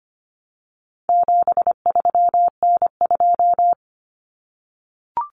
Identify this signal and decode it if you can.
Morse code, heard as a keyed tone, about 25 wpm, 700 Hz. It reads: 73N2
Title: Common Cut Numbers - 4-Digit 25wpm